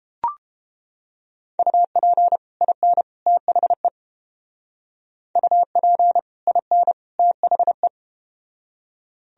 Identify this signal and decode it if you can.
Morse code, heard as a keyed tone, about 31 wpm, 700 Hz. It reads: UP IN THE UP IN THE